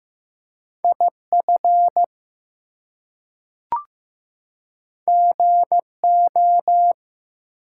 Morse code, heard as a keyed tone, about 15 wpm, 700 Hz. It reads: IF GO
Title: Two Letter Words - 15wpm